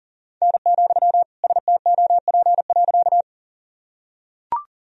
Morse code, heard as a keyed tone, about 40 wpm, 700 Hz. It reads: N, STOP.
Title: Sentences from Top 300 Words and Abbreviations 40wpm